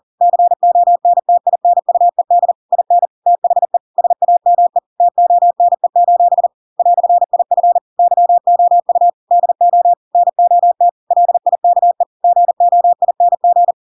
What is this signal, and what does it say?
Morse code, heard as a keyed tone, about 40 wpm, 700 Hz. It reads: CONTINUED IN THE SAME TONE: "IF YOU DO NOT LIKE GOING
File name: Pinocchio / Pinocchio ch4 40 40 ik1hge